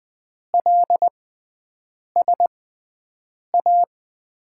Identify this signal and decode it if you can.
Morse code, heard as a keyed tone, about 20 wpm, spaced as for 7 wpm, 700 Hz. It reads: LSA
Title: USA Call Signs - 6x Character Spacing - 20wpm